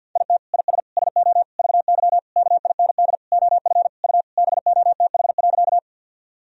Morse code, heard as a keyed tone, about 51 wpm, 700 Hz. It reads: IT IS SO VY KIND OF U BOTH.
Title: Sentences from Top 300 Words and Abbreviations 50wpm